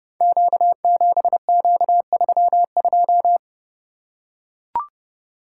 Morse code, heard as a keyed tone, about 30 wpm, 700 Hz. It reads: Q7Q32
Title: Code Groups - 5 Letter:Numbers - 30wpm